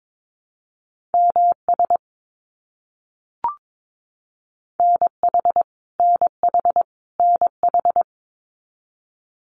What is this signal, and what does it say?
Morse code, heard as a keyed tone, about 22 wpm, 700 Hz. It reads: MS NH NH NH
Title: US States Repeated 3x 22wpm